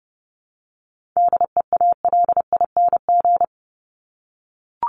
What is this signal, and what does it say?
Morse code, heard as a keyed tone, about 30 wpm, 700 Hz. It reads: DEALING
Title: ING Suffix - Top 100 Words with Base Words 30wpm